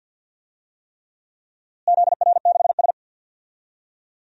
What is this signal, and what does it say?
Morse code, heard as a keyed tone, about 50 wpm, 700 Hz. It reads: ZR6S
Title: DXCC-100 Call Signs - 4 Characters - Repeated 3x - 50wpm